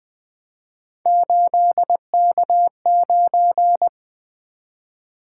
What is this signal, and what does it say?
Morse code, heard as a keyed tone, about 20 wpm, 700 Hz. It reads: 8K9
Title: Three Letter-Number Combinations - 20wpm